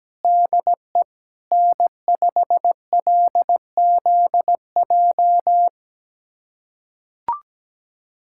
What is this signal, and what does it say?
Morse code, heard as a keyed tone, about 17 wpm, 700 Hz. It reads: DE N5LZJ
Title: USA Callsign DE Callsign - 5 Characters - 17wpm